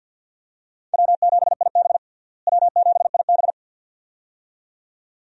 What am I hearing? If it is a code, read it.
Morse code, heard as a keyed tone, about 51 wpm, 700 Hz. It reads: W7IB W7IB
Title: USA Call Signs - 4 Characters - Repeated 2x - 50wpm